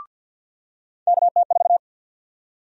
Morse code, heard as a keyed tone, about 50 wpm, 700 Hz. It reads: KT4
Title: USA Call Sign Prefixes - 50wpm